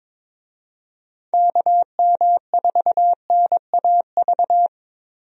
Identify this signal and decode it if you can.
Morse code, heard as a keyed tone, about 21 wpm, 700 Hz. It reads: KM4NAV
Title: USA Call Signs - 6 Characters - Repeated 3x - 22wpm